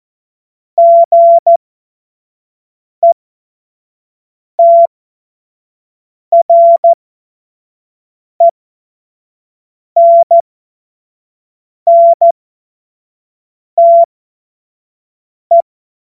Morse code, heard as a keyed tone, about 14 wpm, spaced as for 5 wpm, 680 Hz. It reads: GETRENNTE
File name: Morsekurs / Witze / Witz 15. Februar 2022 25 F70